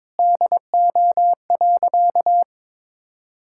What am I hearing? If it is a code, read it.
Morse code, heard as a keyed tone, about 22 wpm, 700 Hz. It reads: DO.
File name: Sentences with 9-Word Phrases - 22wpm